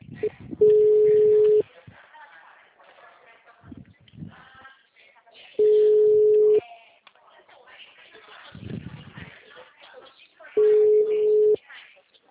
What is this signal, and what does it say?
Ringback tone (Europe, 425 Hz), with speech.